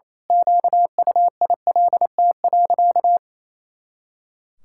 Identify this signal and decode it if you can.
Morse code, heard as a keyed tone, about 28 wpm, 700 Hz. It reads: QUILT.
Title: Pangram Challenge v2 - 28wpm